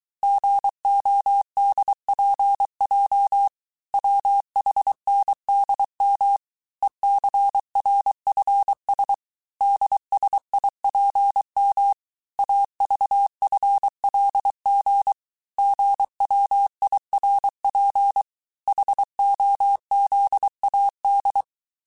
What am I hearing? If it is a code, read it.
Morse code, heard as a keyed tone, about 23 wpm, 800 Hz. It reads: GODPJ WHNDM ECRFS DSIPM AVFLG GWIRP HOZAD
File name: cours-du-mercredi / audios / cours-855-texte-chiffre-vitesse-22-mpm